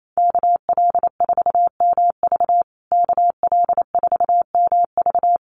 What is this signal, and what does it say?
Morse code, heard as a keyed tone, about 28 wpm, 700 Hz. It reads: KL4MV KL4MV
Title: USA Call Signs Repeated 2x 28wpm